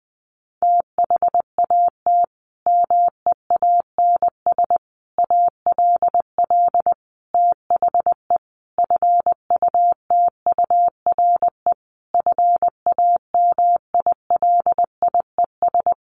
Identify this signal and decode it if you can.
Morse code, heard as a keyed tone, about 20 wpm, 700 Hz. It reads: THAT MEANS ALL THE FUTURE FAMILIES
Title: A Sound Of Thunder 20wpm